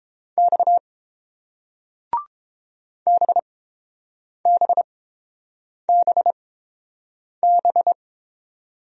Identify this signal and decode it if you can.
Morse code, heard as a keyed tone, about 30 wpm, spaced as for 7 wpm, 690 Hz. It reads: X BBBB